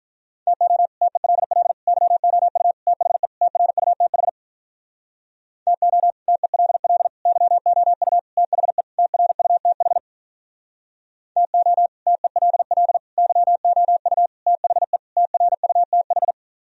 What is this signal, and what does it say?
Morse code, heard as a keyed tone, about 46 wpm, 700 Hz. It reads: TO TELL YOU THE TRUTH TO TELL YOU THE TRUTH TO TELL YOU THE TRUTH